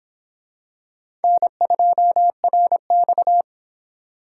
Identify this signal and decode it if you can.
Morse code, heard as a keyed tone, about 24 wpm, 700 Hz. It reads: N2RX